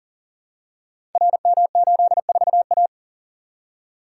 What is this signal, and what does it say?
Morse code, heard as a keyed tone, about 38 wpm, 700 Hz. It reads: RM84A
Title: Code Groups - 5 Letter:Numbers - 40wpm